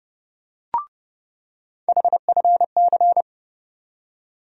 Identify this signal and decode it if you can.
Morse code, heard as a keyed tone, about 30 wpm, 700 Hz. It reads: HFC
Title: USA Call Sign Postfixes - 3 Letters 30wpm